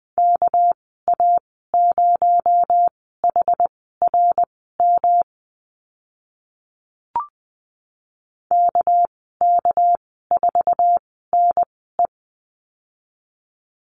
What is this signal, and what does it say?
Morse code, heard as a keyed tone, about 20 wpm, spaced as for 14 wpm, 700 Hz. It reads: KA0HRM KK4NE
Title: 231 Lesson 37g - Letter x - Call Signs 20wpm